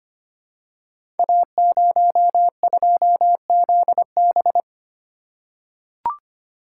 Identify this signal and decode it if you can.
Morse code, heard as a keyed tone, about 25 wpm, 700 Hz. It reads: A02ZB
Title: Code Groups - 5 Letter:Numbers - 25wpm